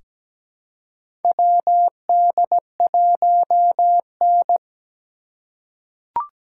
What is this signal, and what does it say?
Morse code, heard as a keyed tone, about 17 wpm, 700 Hz. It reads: WD1N
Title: USA Call Signs - 2x Character Spacing - 17wpm